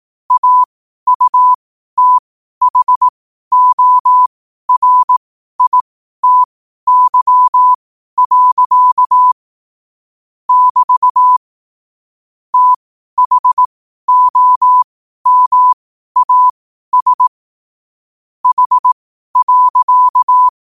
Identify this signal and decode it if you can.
Morse code, heard as a keyed tone, about 18 wpm, spaced as for 12 wpm, 1000 Hz. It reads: AUTHORITY. <BT> THOMAS H.
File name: qotdfiles / 12wpm